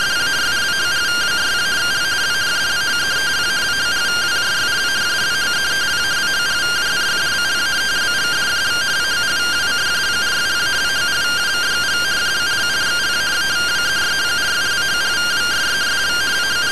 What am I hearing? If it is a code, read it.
Two-tone teleprinter signal, about 170 Hz shift.